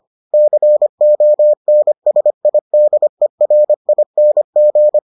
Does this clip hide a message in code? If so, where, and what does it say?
Yes: Morse code from 0.3 to 5.0 s, heard as a keyed tone, about 25 wpm, 600 Hz. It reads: CONSIDERING